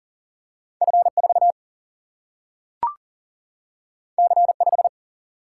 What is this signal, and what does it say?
Morse code, heard as a keyed tone, about 39 wpm, 700 Hz. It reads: F4 C5
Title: Single-Letter Single-Number 40wpm